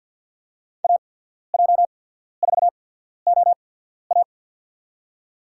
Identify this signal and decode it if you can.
Morse code, heard as a keyed tone, about 53 wpm, spaced as for 14 wpm, 700 Hz. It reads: AJ4OA
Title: USA Call Signs - 5 Characters - 8x Character Spacing - 50wpm